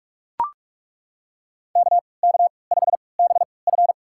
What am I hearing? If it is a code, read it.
Morse code, heard as a keyed tone, about 42 wpm, spaced as for 25 wpm, 700 Hz. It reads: KK5BF
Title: USA Call Signs - 5 Characters - 3x Character Spacing - 45wpm